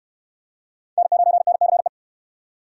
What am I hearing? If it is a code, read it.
Morse code, heard as a keyed tone, about 64 wpm, 700 Hz. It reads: N9NCE